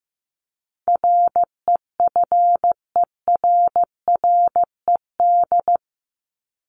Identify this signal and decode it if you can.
Morse code, heard as a keyed tone, about 15 wpm, 700 Hz. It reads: REFERRED
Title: ED Suffix - Top 100 Words with Base Words 15wpm